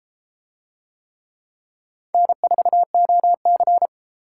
Morse code, heard as a keyed tone, about 32 wpm, 700 Hz. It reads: N4OC